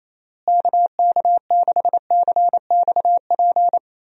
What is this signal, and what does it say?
Morse code, heard as a keyed tone, about 28 wpm, 700 Hz. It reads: KK6CXP
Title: USA Call Signs - 6 Characters - Repeated 2x - 28wpm